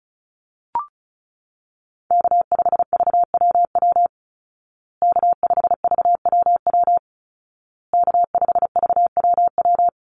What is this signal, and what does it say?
Morse code, heard as a keyed tone, about 34 wpm, 700 Hz. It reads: K5VWW K5VWW K5VWW
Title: Call Signs Repeated 2x - 35wpm